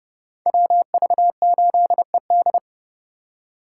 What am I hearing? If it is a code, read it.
Morse code, heard as a keyed tone, about 30 wpm, 700 Hz. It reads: WV8ED